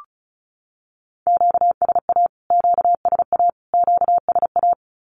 Morse code, heard as a keyed tone, about 34 wpm, 700 Hz. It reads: QSA QSA QSA
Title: Q-Codes Repeated 3x 35wpm